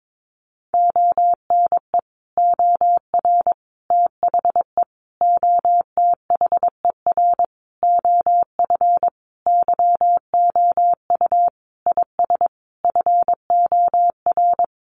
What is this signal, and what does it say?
Morse code, heard as a keyed tone, about 22 wpm, 700 Hz. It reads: ONE OR THE OTHER OF YOU IS FOR
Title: Sentences from Top 400 Words 22wpm